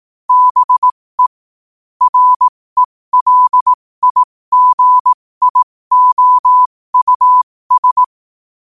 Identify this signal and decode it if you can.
Morse code, heard as a keyed tone, about 18 wpm, spaced as for 15 wpm, 1000 Hz. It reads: BE RELIGIOUS
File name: qotdfiles / 15wpm